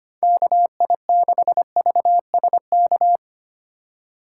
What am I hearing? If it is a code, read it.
Morse code, heard as a keyed tone, about 32 wpm, 700 Hz. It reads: KI6VSK